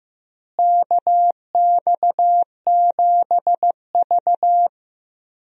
Morse code, heard as a keyed tone, about 15 wpm, 700 Hz. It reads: KX7V
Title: USA Call Signs - 4 Characters - Repeated 3x - 15wpm